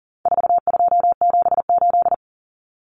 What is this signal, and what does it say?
Morse code, heard as a keyed tone, about 40 wpm, 700 Hz. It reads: 4278
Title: Numbers - 4-Digit 40wpm